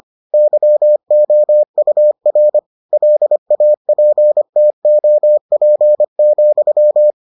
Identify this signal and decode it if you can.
Morse code, heard as a keyed tone, about 25 wpm, 600 Hz. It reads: YOUR LAPTOP,